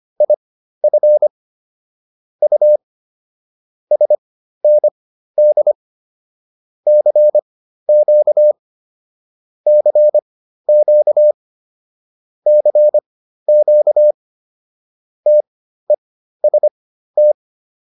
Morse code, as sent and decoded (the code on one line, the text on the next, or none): .. ..-. / ..- / ... -. -.. / -.-. --.- / -.-. --.- / -.-. --.- / - . ... -
IF U SND CQ CQ CQ TEST